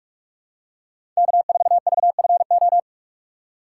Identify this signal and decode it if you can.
Morse code, heard as a keyed tone, about 45 wpm, 700 Hz. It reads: K4VFO
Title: USA Call Signs - 2x Character Spacing - 45wpm